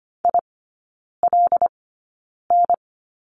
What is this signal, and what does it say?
Morse code, heard as a keyed tone, about 25 wpm, spaced as for 9 wpm, 700 Hz. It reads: ILN
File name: Lesson 15e - Number 5 - Three Characters 25wpm